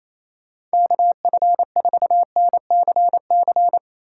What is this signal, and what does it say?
Morse code, heard as a keyed tone, about 34 wpm, 700 Hz. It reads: KF4NCC
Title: Speed Racing - USA 2x3 Call Signs 28wpm